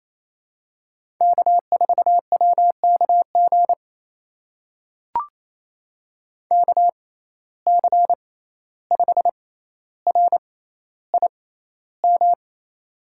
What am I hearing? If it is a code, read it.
Morse code, heard as a keyed tone, about 28 wpm, spaced as for 14 wpm, 700 Hz. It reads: K4WKG KC5RIM
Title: USA Call Signs - 6x Character Spacing - 28wpm